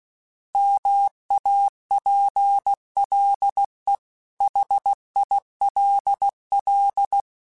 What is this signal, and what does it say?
Morse code, heard as a keyed tone, about 16 wpm, 800 Hz. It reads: MAPLE HILL
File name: fromSMs / FromSMs mp3s / Cities@15